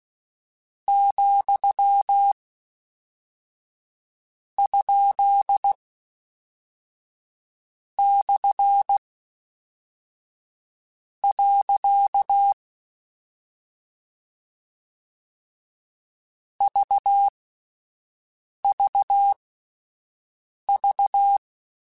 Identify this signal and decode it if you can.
Morse code, heard as a keyed tone, about 16 wpm, spaced as for 4 wpm, 800 Hz. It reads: ,?/. VVV
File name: Misc / Misc-mp3s / Punct @5